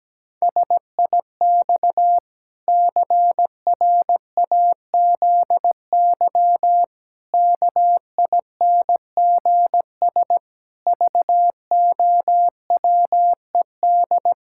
Morse code, heard as a keyed tone, about 17 wpm, 700 Hz. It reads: SIX CRAZY KINGS VOWED
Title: Pangram Challenge v2 - 17wpm